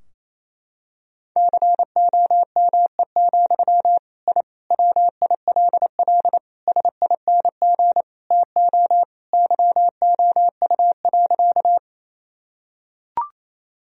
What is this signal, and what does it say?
Morse code, heard as a keyed tone, about 27 wpm, 700 Hz. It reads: COME, I WILL SING TO YOU.
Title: Sentences from Top 1000 Words - Review Part 2 - 28wpm